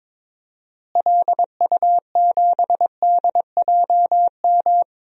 Morse code, heard as a keyed tone, about 22 wpm, 700 Hz. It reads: LU7DJM
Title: DXCC-100 Call Signs - 6 Characters - Repeated 2x - 22wpm